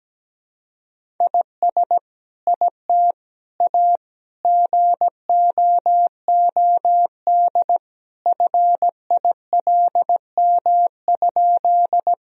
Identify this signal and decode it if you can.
Morse code, heard as a keyed tone, about 17 wpm, 700 Hz. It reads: IS IT A GOOD FILM?
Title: Sentences from Top 1000 Words - Review Part 3 - 17wpm